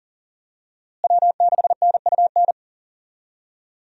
Morse code, heard as a keyed tone, about 40 wpm, 700 Hz. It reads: W6NUN
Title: USA Call Signs - 5 Characters - Repeated 2x - 40wpm